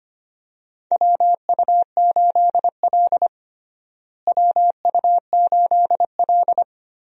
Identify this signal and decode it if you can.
Morse code, heard as a keyed tone, about 25 wpm, 700 Hz. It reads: WU8L WU8L
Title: USA Call Signs - 4 Characters - Repeated 3x - 25wpm